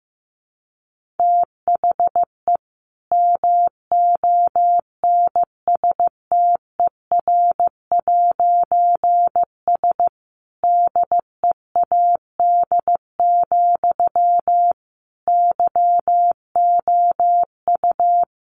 Morse code, heard as a keyed tone, about 15 wpm, 700 Hz. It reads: THE MONSTER'S DEAD, YOU
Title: A Sound Of Thunder 15wpm